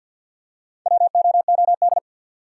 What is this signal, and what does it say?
Morse code, heard as a keyed tone, about 53 wpm, 700 Hz. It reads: WOOD
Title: Four Letter Words - 50wpm